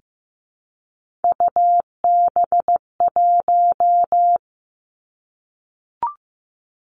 Morse code, heard as a keyed tone, about 15 wpm, 700 Hz. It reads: UB1
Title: Three Challenging Letter-Number Combinations 15wpm